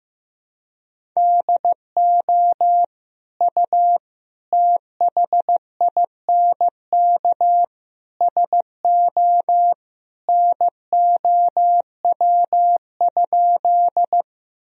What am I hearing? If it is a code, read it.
Morse code, heard as a keyed tone, about 15 wpm, 700 Hz. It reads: DO U THINK SO NOW?
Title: Sentences from Top 200 Words and Abbreviations 15wpm